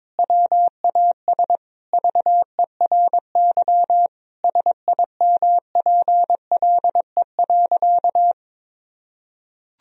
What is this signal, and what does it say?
Morse code, heard as a keyed tone, about 22 wpm, 700 Hz. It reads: WAS VERY SIMPLE.
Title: Sentences from Top 1000 Words - Review Part 4 - 22wpm